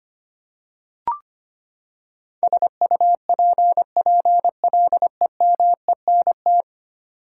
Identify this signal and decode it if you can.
Morse code, heard as a keyed tone, about 25 wpm, 700 Hz. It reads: SUPPLEMENT
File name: Ten Letter Words - 25wpm